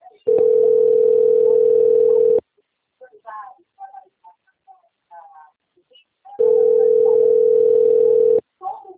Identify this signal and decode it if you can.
Ringback tone (North America), 440 + 480 Hz, with speech.
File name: out- - - - - .5275